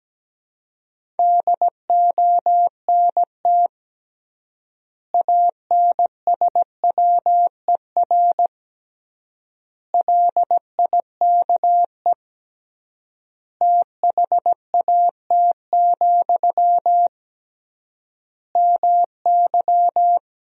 Morse code, heard as a keyed tone, about 17 wpm, 700 Hz. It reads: DONT ANSWER LIKE THAT, MY
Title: 3x Word-Spacing - Sentences from Top 400 Words - 17wpm